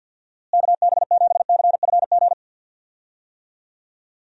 Morse code, heard as a keyed tone, about 52 wpm, 700 Hz. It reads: KB7CFG
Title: USA Call Signs - 6 Characters - Repeated 3x - 50wpm